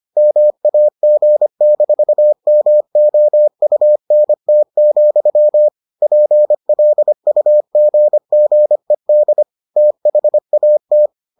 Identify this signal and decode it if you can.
Morse code, heard as a keyed tone, about 25 wpm, 600 Hz. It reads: MAG-MOUNT, PLUGGED THAT